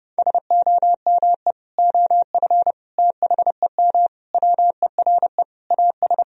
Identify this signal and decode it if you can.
Morse code, heard as a keyed tone, about 30 wpm, 700 Hz. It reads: SOME OF THEM WERE AS
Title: Sentences from Top 1000 Words - Review Part 1 - 30wpm